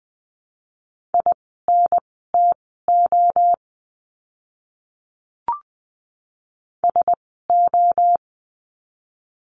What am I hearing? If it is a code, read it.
Morse code, heard as a keyed tone, about 20 wpm, spaced as for 15 wpm, 700 Hz. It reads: INTO SO
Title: 230 Lesson 37f - Letter x - Words 20wpm